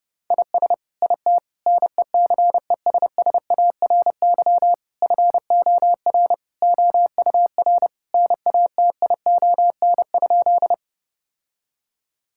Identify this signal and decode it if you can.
Morse code, heard as a keyed tone, about 32 wpm, 700 Hz. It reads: IS IT NECESSARY FOR OUR NATION?
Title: Sentences from Top 1000 Words - Review Part 6 - 30wpm